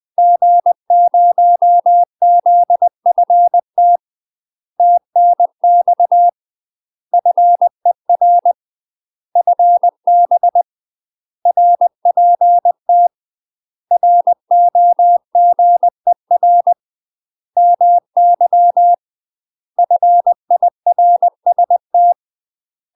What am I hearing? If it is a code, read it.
Morse code, heard as a keyed tone, about 20 wpm, 700 Hz. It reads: G0ZFT TNX FER FB RPT ROGER MY FIRST
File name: UK MASTER QSOs TRAINING / 20WPM / UK MASTER QSOs 0038